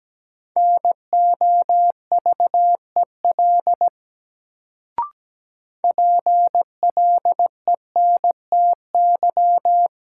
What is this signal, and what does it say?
Morse code, heard as a keyed tone, about 17 wpm, 700 Hz. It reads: NOVEL PLENTY
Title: Sentences from Top 1300 Words - New Word Drill - 17wpm